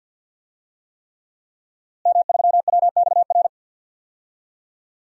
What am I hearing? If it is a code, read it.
Morse code, heard as a keyed tone, about 46 wpm, 700 Hz. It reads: M3WXR
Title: DXCC-100 Call Signs - 5 Characters - Repeated 3x - 50wpm